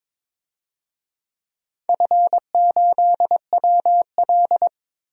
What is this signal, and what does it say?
Morse code, heard as a keyed tone, about 21 wpm, 700 Hz. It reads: F8WL
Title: DXCC-100 Call Signs - 4 Characters - 22wpm